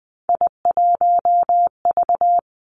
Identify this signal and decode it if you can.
Morse code, heard as a keyed tone, about 20 wpm, 700 Hz. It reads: I1V
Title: Three Challenging Letter-Number Combinations 20wpm